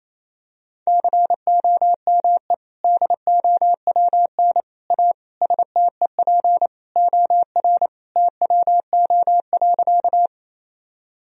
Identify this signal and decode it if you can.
Morse code, heard as a keyed tone, about 28 wpm, 700 Hz. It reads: COME DOWN A STEP OR TWO.
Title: Sentences from Top 1000 Words - Review Part 2 - 28wpm